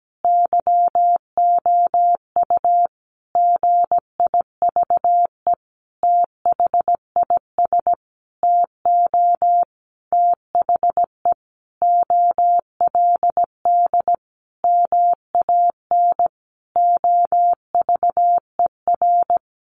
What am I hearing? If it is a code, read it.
Morse code, heard as a keyed tone, about 17 wpm, 700 Hz. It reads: YOU GIVE THIS TO THE OLD MAN OVER
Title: Sentences from Top 200 Words 17wpm